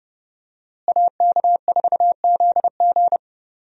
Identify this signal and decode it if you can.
Morse code, heard as a keyed tone, about 30 wpm, 700 Hz. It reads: AK4ZG
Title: USA Call Signs - 5 Characters - Repeated 2x - 30wpm